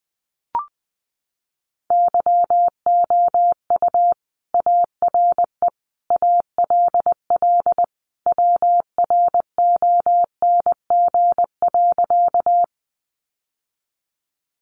Morse code, heard as a keyed tone, about 20 wpm, 700 Hz. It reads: YOU ARE ALL WRONG.
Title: Sentences from Top 900 Words 20wpm